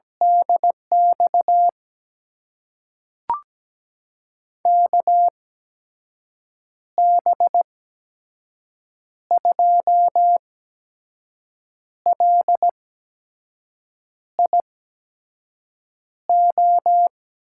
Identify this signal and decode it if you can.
Morse code, heard as a keyed tone, about 17 wpm, spaced as for 5 wpm, 700 Hz. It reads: DX KB2LIO
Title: USA Call Signs - 6 Characters - 8x Character Spacing - 17wpm